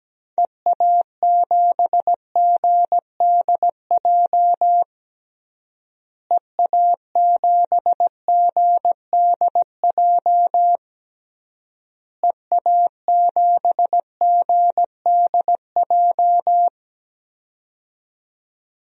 Morse code, heard as a keyed tone, about 17 wpm, 700 Hz. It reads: EA7GDJ EA7GDJ EA7GDJ
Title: DXCC-100 Call Signs - 6 Characters - Repeated 3x - 17wpm